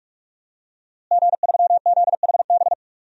Morse code, heard as a keyed tone, about 46 wpm, 700 Hz. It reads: G3ZHB
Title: DXCC-100 Call Signs - 5 Characters - Repeated 3x - 45wpm